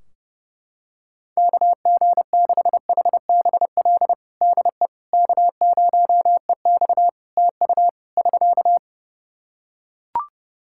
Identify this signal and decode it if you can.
Morse code, heard as a keyed tone, about 29 wpm, 700 Hz. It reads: KG6HBL DE K0EX TU <SK>
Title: K1USN SST QSO Practice 30wpm